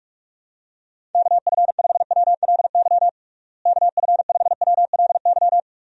Morse code, heard as a keyed tone, about 46 wpm, 700 Hz. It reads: KF5WLY KF5WLY
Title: USA Call Signs - 6 Characters - Repeated 3x - 45wpm